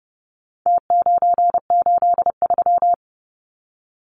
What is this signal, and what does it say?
Morse code, heard as a keyed tone, about 30 wpm, 700 Hz. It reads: T983
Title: Numbers - Contest Style - 4-Digit 30wpm